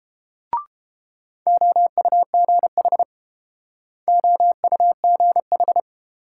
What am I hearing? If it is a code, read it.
Morse code, heard as a keyed tone, about 30 wpm, 700 Hz. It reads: OUGH OUGH